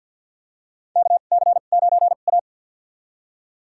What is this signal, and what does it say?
Morse code, heard as a keyed tone, about 48 wpm, spaced as for 35 wpm, 700 Hz. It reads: KC9A
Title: USA Call Signs - 4 Characters - 2x Character Spacing - 50wpm